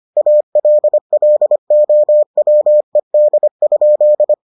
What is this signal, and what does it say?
Morse code, heard as a keyed tone, about 25 wpm, 600 Hz. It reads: ALLOWED?